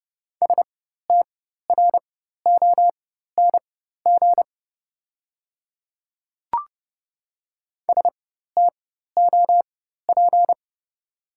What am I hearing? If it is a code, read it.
Morse code, heard as a keyed tone, about 31 wpm, spaced as for 14 wpm, 700 Hz. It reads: STRONG STOP
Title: 4x Character Spacing - Top 500 Words - 30wpm